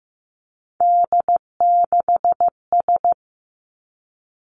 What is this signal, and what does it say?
Morse code, heard as a keyed tone, about 15 wpm, 700 Hz. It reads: D6S
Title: Three Challenging Letter-Number Combinations 15wpm